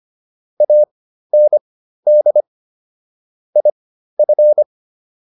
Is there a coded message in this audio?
Yes: Morse code from 0.6 to 4.6 s, heard as a keyed tone, about 25 wpm, spaced as for 13 wpm, 600 Hz. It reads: AND IF